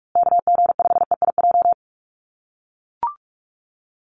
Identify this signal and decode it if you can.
Morse code, heard as a keyed tone, about 44 wpm, 700 Hz. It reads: KG5EIJ
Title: USA Call Signs Repeated 3x 45wpm